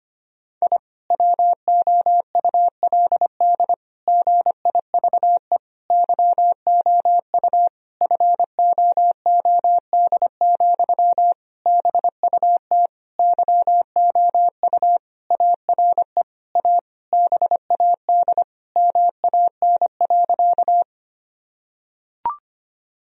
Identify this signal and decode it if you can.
Morse code, heard as a keyed tone, about 25 wpm, 700 Hz. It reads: I WOULD GIVE YOU FOOD, BUT YOU ARE A BAD MAN.
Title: Sentences from Top 1000 Words - Review Part 1 - 25wpm